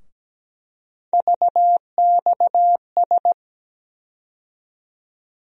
Morse code, heard as a keyed tone, about 17 wpm, 700 Hz. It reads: VXS
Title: USA Call Sign Postfixes - 3 Letters 17wpm